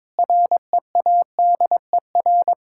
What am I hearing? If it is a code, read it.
Morse code, heard as a keyed tone, about 22 wpm, 700 Hz. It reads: READER
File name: Difficult Words with Hints - 22wpm